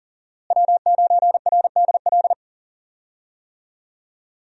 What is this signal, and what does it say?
Morse code, heard as a keyed tone, about 37 wpm, 700 Hz. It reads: W9RDL
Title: USA Call Signs - 5 Characters - Repeated 2x - 40wpm